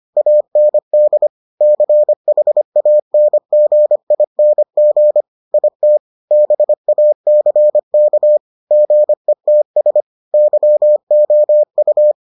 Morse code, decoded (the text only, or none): AND CHANGING IT BACK GETS YOU